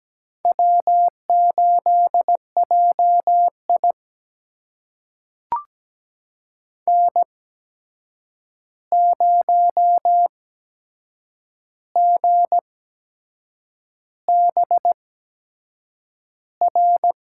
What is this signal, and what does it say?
Morse code, heard as a keyed tone, about 17 wpm, spaced as for 7 wpm, 700 Hz. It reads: W8JI N0GBR
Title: USA Call Signs - 8x Character Spacing - 17wpm